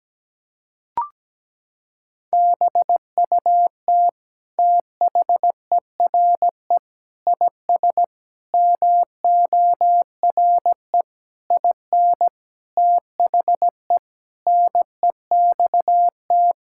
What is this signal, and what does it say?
Morse code, heard as a keyed tone, about 17 wpm, 700 Hz. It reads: BUT THERE IS MORE IN THE NEXT
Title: Sentences from Top 1000 Words - Review Part 5 - 17wpm